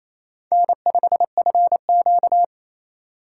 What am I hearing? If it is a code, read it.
Morse code, heard as a keyed tone, about 28 wpm, 700 Hz. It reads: N5FQ
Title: USA Call Signs - 4 Characters - 6x Character Spacing - 28wpm